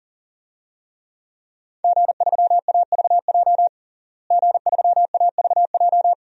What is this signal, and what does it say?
Morse code, heard as a keyed tone, about 40 wpm, 700 Hz. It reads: G3AVJ G3AVJ
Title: DXCC-100 Call Signs - 5 Characters - Repeated 2x - 40wpm